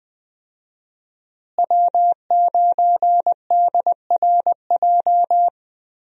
Morse code, heard as a keyed tone, about 21 wpm, 700 Hz. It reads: W9DRJ
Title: USA Call Signs - 5 Characters - Repeated 3x - 20wpm